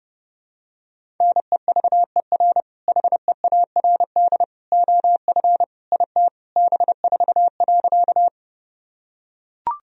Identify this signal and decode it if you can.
Morse code, heard as a keyed tone, about 32 wpm, 700 Hz. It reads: NEVER HEARD OF IT B4.
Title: Sentences from Top 400 Words and Abbreviations 30wpm